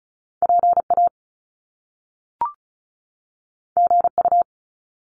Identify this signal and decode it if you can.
Morse code, heard as a keyed tone, about 36 wpm, 700 Hz. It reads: PA GU
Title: Top Two Letter Combinations in Words 35wpm